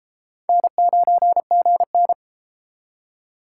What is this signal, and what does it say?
Morse code, heard as a keyed tone, about 31 wpm, 700 Hz. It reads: N9GN